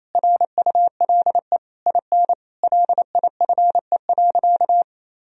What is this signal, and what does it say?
Morse code, heard as a keyed tone, about 28 wpm, 700 Hz. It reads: RULE IN LIFE.
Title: Sentences from Top 1000 Words - Review Part 2 - 28wpm